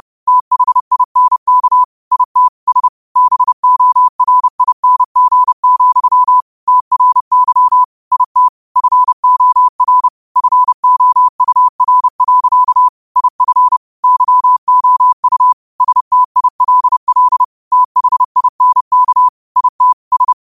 Morse code, heard as a keyed tone, about 30 wpm, 1000 Hz. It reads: THINK IT S BORING, TRY IT FOR FOUR. IF YOU STILL THINK IT S
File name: qotdfiles / 30wpm